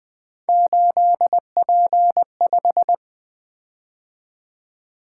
Morse code, heard as a keyed tone, about 20 wpm, 700 Hz. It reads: 8P5
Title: Three Letter-Number Combinations - 20wpm